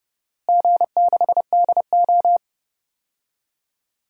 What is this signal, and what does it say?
Morse code, heard as a keyed tone, about 30 wpm, 700 Hz. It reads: G6DO
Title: DXCC-100 Call Signs - 4 Characters - Repeated 3x - 30wpm